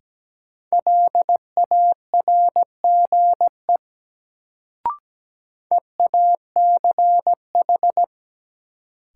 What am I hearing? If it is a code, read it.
Morse code, heard as a keyed tone, about 17 wpm, 700 Hz. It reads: LARGE EACH
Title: Sentences from Top 200 Words - New Word Drill - 17wpm